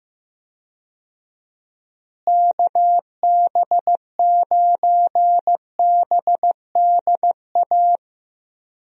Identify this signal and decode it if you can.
Morse code, heard as a keyed tone, about 15 wpm, 700 Hz. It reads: KB9BDA
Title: USA Call Signs - 6 Characters - Repeated 2x - 15wpm